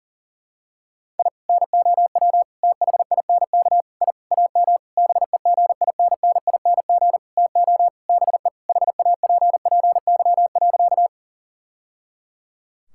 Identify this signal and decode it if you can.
Morse code, heard as a keyed tone, about 40 wpm, 700 Hz. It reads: I NOW THINK I AM BEGINNING TO BE HAPPY.
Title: Sentences from Top 1000 Words - Review Part 5 - 40wpm